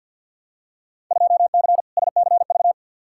Morse code, heard as a keyed tone, about 48 wpm, 700 Hz. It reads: 2C SCV
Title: Field Day Exchanges Encore - 50wpm